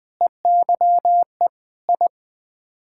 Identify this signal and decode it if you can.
Morse code, heard as a keyed tone, about 20 wpm, 700 Hz. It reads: EYE I
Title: Sets of 8 Words - 20wpm